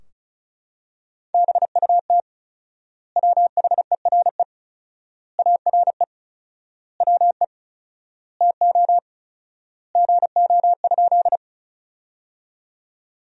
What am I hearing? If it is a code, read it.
Morse code, heard as a keyed tone, about 36 wpm, 700 Hz. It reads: BUT WHERE ARE WE TO GO?
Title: 4x Word-Spacing - Sentences from Top 200 Words - 35wpm